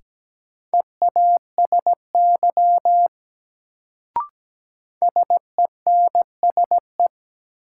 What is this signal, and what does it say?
Morse code, heard as a keyed tone, about 17 wpm, 700 Hz. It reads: EASY SENSE
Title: Sentences from Top 700 Words - New Word Drill - 17wpm